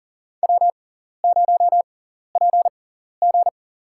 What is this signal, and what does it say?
Morse code, heard as a keyed tone, about 44 wpm, spaced as for 14 wpm, 700 Hz. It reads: W0PG
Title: USA Call Signs - 4 Characters - 6x Character Spacing - 40wpm